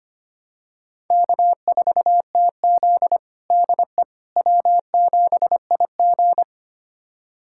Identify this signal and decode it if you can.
Morse code, heard as a keyed tone, about 25 wpm, 700 Hz. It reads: K4TZ DE W7IG
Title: USA Callsign DE Callsign - 4 Characters - 25wpm